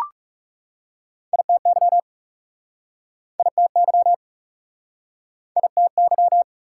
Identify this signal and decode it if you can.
Morse code, heard as a keyed tone, about 42 wpm, 690 Hz. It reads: ITY ITY ITY